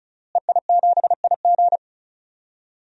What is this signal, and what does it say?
Morse code, heard as a keyed tone, about 36 wpm, 700 Hz. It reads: EI7IG
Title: DXCC-100 Call Signs - 5 Characters - Repeated 2x - 35wpm